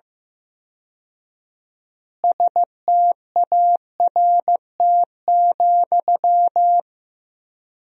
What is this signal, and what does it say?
Morse code, heard as a keyed tone, about 15 wpm, 700 Hz. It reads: START,
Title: 4x Word-Spacing - Sentences from Top 300 Words - 15wpm